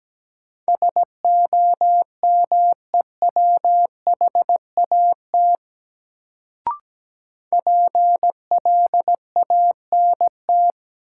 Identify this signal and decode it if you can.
Morse code, heard as a keyed tone, about 17 wpm, 700 Hz. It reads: SOMEWHAT PLANT
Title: Sentences from Top 1000 Words - Review Part 1 - New Word Drill - 17wpm